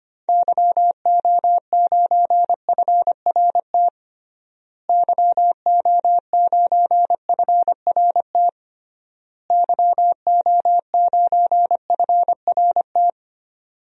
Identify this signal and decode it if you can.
Morse code, heard as a keyed tone, about 25 wpm, 690 Hz. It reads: YO9FRT YO9FRT YO9FRT
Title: DXCC-100 Call Signs - 6 Characters - Repeated 3x - 25wpm